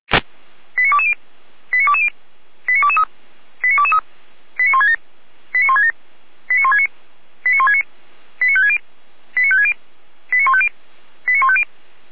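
Five-tone selective call (ZVEI), heard as quick runs of five tones, tones of about 70 ms. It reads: 89220 89220 89223 89223 89167 89167 89169 89169 89560 89560 89150 89150
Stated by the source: ZVEI five-tone selective call received by SDR.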